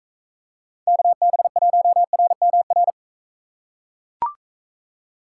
Morse code, heard as a keyed tone, about 38 wpm, 700 Hz. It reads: KB1RMR